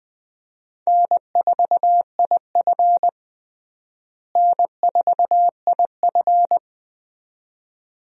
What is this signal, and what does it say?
Morse code, heard as a keyed tone, about 20 wpm, 700 Hz. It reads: N4IF N4IF
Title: USA Call Signs - 4 Characters - Repeated 2x - 20wpm